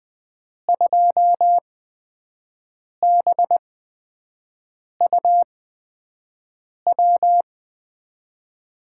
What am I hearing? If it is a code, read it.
Morse code, heard as a keyed tone, about 21 wpm, spaced as for 5 wpm, 700 Hz. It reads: 2BUW
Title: USA Call Signs - 6 Characters - 8x Character Spacing - 20wpm